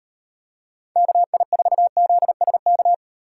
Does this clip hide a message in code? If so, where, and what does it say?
Yes: Morse code from 1.0 to 2.9 s, heard as a keyed tone, about 38 wpm, 700 Hz. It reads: KI4ZSK